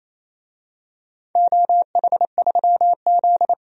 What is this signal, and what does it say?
Morse code, heard as a keyed tone, about 28 wpm, 700 Hz. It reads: OH3Z
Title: DXCC-100 Call Signs - 4 Characters - 28wpm